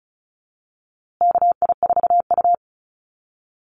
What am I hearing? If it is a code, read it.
Morse code, heard as a keyed tone, about 34 wpm, 700 Hz. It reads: KI4U
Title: USA Call Signs Repeated 3x 35wpm